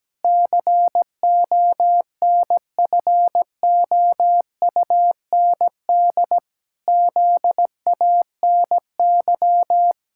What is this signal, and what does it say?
Morse code, heard as a keyed tone, about 17 wpm, 700 Hz. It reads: CONFOUND ZANY
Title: Pangram Challenge v2 - 17wpm